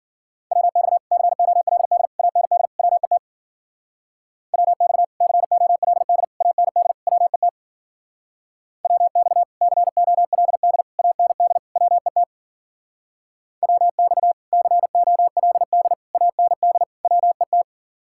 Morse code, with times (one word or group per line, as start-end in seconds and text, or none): 0.5-1.0 WX
1.1-2.1 COLD
2.2-2.7 AND
2.8-3.2 WET
4.5-5.0 WX
5.2-6.2 COLD
6.4-6.9 AND
7.1-7.5 WET
8.8-9.4 WX
9.6-10.8 COLD
11.0-11.6 AND
11.8-12.2 WET
13.6-14.3 WX
14.5-15.9 COLD
16.1-16.8 AND
17.1-17.6 WET